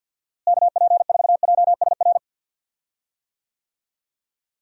Morse code, heard as a keyed tone, about 47 wpm, 700 Hz. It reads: KP4JSR
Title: USA Call Signs - 6 Characters - Repeated 2x - 50wpm